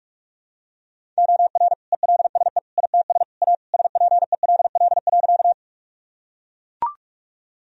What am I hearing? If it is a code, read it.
Morse code, heard as a keyed tone, about 41 wpm, 700 Hz. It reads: OR ELSE ITS A SPELL.
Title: Sentences from Top 1000 Words - Review Part 1 - 45wpm